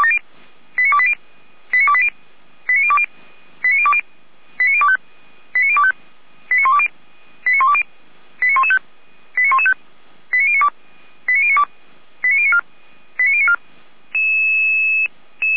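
Five-tone selective call (ZVEI), heard as quick runs of five tones, tones of about 70 ms. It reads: 89280 89280 89020 89020 89025 89025 89120 89120 89115 89115 89092 89092 89094 89094
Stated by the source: ZVEI five-tone selective call received by SDR, from the feuerwehr-orlamuende fire brigade audio archive.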